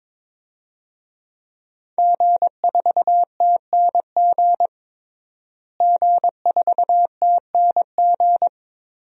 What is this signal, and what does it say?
Morse code, heard as a keyed tone, about 21 wpm, 700 Hz. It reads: G4TNG G4TNG
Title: DXCC-100 Call Signs - 5 Characters - Repeated 2x - 22wpm